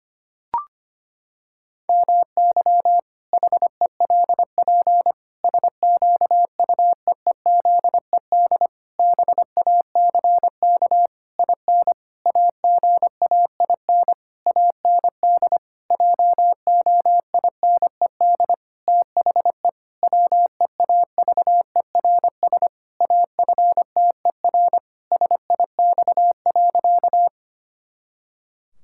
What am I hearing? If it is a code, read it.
Morse code, heard as a keyed tone, about 25 wpm, 700 Hz. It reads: MY HELP SQUEEZED BACK IN AGAIN AND JOINED THE WEAVERS AFTER SIX.
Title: Pangram Challenge v2 - 25wpm